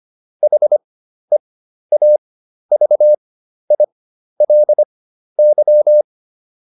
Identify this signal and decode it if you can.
Morse code, heard as a keyed tone, about 25 wpm, spaced as for 12 wpm, 600 Hz. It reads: HEAVILY